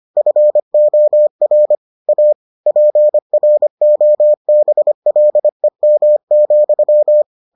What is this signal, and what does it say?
Morse code, heard as a keyed tone, about 25 wpm, 600 Hz. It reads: FOR A PROBLEM,